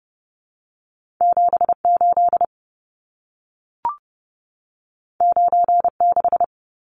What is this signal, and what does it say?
Morse code, heard as a keyed tone, about 30 wpm, 700 Hz. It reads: 78 96
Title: Numbers - 2-Digit 30wpm